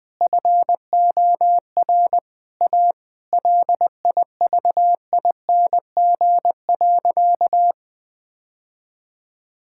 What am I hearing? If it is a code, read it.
Morse code, heard as a keyed tone, about 20 wpm, 700 Hz. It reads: FOR A LIVING.
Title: Sentences from Top 1000 Words - Review Part 6 - 20wpm